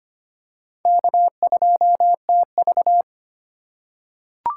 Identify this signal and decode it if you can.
Morse code, heard as a keyed tone, about 25 wpm, 700 Hz. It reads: K2TV
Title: K1USN SST QSO Practice - 25wpm